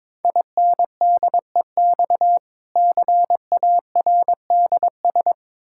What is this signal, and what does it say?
Morse code, heard as a keyed tone, about 22 wpm, 690 Hz. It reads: INDEX CARDS
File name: Difficult Words with Hints - 22wpm